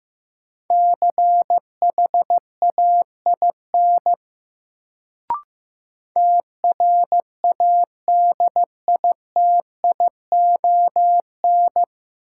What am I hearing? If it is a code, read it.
Morse code, heard as a keyed tone, about 16 wpm, 700 Hz. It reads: CHAIN TRADITION
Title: Sentences from Top 1500 Words - New Word Drill - 15wpm